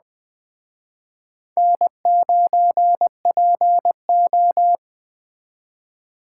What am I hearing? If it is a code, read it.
Morse code, heard as a keyed tone, about 20 wpm, 700 Hz. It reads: N9PO
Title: Speed Racing - USA 1x2 Call Signs 17wpm